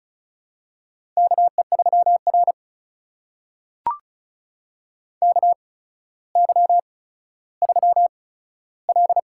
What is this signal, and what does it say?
Morse code, heard as a keyed tone, about 36 wpm, spaced as for 15 wpm, 700 Hz. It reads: KE3R KY3L
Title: USA Call Signs - 4 Characters - 8x Character Spacing - 35wpm